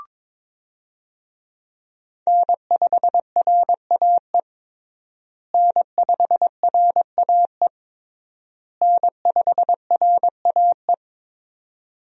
Morse code, heard as a keyed tone, about 22 wpm, 700 Hz. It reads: N5RAE N5RAE N5RAE
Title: USA Call Signs - 5 Characters - Repeated 3x - 22wpm